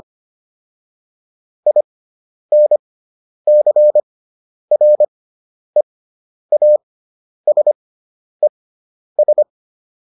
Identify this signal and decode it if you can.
Morse code, heard as a keyed tone, about 25 wpm, spaced as for 10 wpm, 600 Hz. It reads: INCREASES